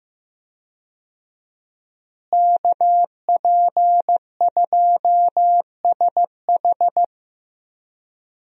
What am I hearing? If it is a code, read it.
Morse code, heard as a keyed tone, about 15 wpm, 700 Hz. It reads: KP2SH
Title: DXCC-100 Call Signs - 5 Characters - Repeated 2x - 15wpm